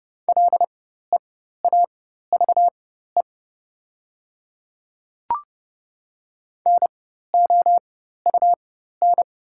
Morse code, heard as a keyed tone, about 31 wpm, spaced as for 14 wpm, 690 Hz. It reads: LEAVE NOUN
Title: 4x Character Spacing - Top 500 Words - 30wpm